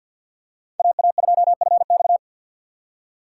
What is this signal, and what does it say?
Morse code, heard as a keyed tone, about 53 wpm, 700 Hz. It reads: AA2FX
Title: USA Call Signs - 5 Characters - 8x Character Spacing - 50wpm